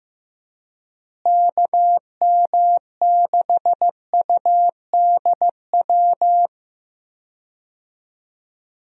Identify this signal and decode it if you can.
Morse code, heard as a keyed tone, about 15 wpm, 700 Hz. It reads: KM6UDW
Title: USA Call Signs - 6 Characters - Repeated 3x - 15wpm